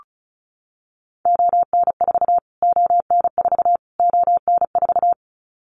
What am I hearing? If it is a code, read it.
Morse code, heard as a keyed tone, about 34 wpm, 700 Hz. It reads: ON4 ON4 ON4
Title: Callsign Prefixes for Top 25 DXCC - Repeated 3x - 35wpm